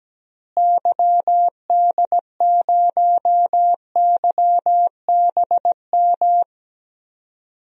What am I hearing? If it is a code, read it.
Morse code, heard as a keyed tone, about 17 wpm, 700 Hz. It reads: YD0YBM
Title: DXCC-100 Call Signs - 6 Characters - Repeated 3x - 17wpm